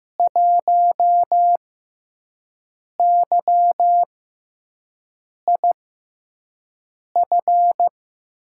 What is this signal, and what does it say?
Morse code, heard as a keyed tone, about 15 wpm, spaced as for 5 wpm, 700 Hz. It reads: 1YIF
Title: USA Call Signs - 6x Character Spacing - 15wpm